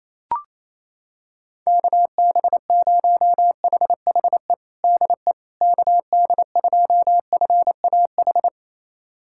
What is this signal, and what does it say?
Morse code, heard as a keyed tone, about 28 wpm, 700 Hz. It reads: KB0HHE DE KD2FAH
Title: USA Callsign DE Callsign 28wpm